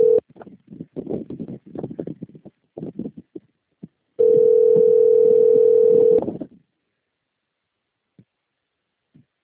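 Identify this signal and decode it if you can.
Ringback tone (North America), 440 + 480 Hz.
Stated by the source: Outbound telephone call recorded on a system labelled Sao Carlos - SP.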